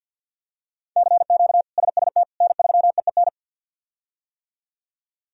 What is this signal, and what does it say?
Morse code, heard as a keyed tone, about 52 wpm, 690 Hz. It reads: CQ SST N3EEN
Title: K1USN SST QSO Practice - 50wpm